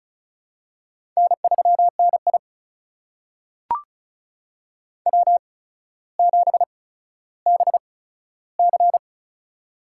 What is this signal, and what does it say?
Morse code, heard as a keyed tone, about 36 wpm, spaced as for 15 wpm, 700 Hz. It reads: N3NI W7BC
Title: USA Call Signs - 4 Characters - 8x Character Spacing - 35wpm